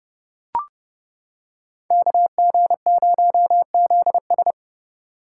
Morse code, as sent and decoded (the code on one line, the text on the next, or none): -.- --. ----- --.. ...
KG0ZS